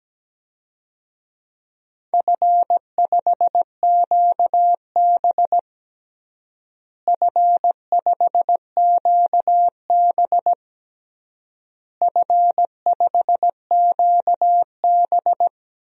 Morse code, heard as a keyed tone, about 17 wpm, 700 Hz. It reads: F5QB F5QB F5QB
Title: DXCC-100 Call Signs - 4 Characters - Repeated 3x - 17wpm